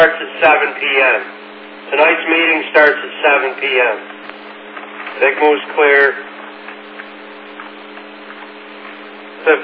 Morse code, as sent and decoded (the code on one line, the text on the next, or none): none